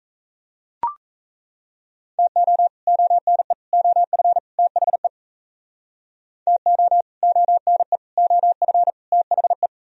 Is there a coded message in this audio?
Yes: Morse code from 2.2 to 9.7 s, heard as a keyed tone, about 38 wpm, 700 Hz. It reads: TO ONE OF THE TO ONE OF THE